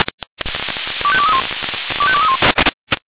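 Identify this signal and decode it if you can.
Five-tone selective call (ZVEI), heard as quick runs of five tones, tones of about 70 ms. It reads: 25231 25231
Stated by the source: ZVEI five-tone selective call received on SDR.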